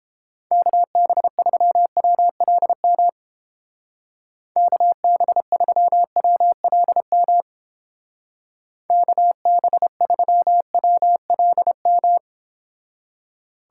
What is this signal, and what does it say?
Morse code, heard as a keyed tone, about 29 wpm, 700 Hz. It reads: KB3WLM KB3WLM KB3WLM